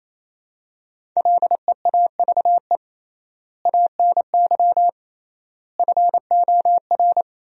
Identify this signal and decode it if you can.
Morse code, heard as a keyed tone, about 28 wpm, 700 Hz. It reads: LEAVE ANY FOR
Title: 3x Word-Spacing - Sentences from Top 400 Words - 28wpm